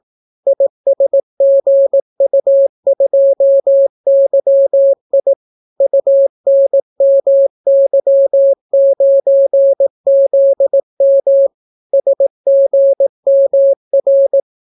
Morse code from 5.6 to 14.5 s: UNMY9ZM SGMR